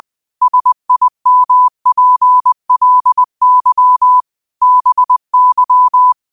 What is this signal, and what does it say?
Morse code, heard as a keyed tone, about 20 wpm, 1000 Hz. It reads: SIMPLY BY
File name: qotdfiles / 20wpm